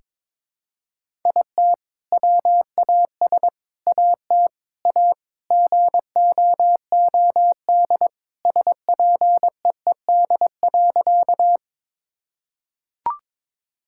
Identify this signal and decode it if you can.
Morse code, heard as a keyed tone, about 22 wpm, 700 Hz. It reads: IT WAS AT A GOOD SPEED.
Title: Sentences from Top 1000 Words - Review Part 5 - 22wpm